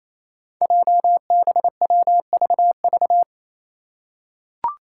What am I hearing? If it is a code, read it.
Morse code, heard as a keyed tone, about 28 wpm, 700 Hz. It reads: JBWVV
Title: Five Challenging Letter-Number Combinations 28wpm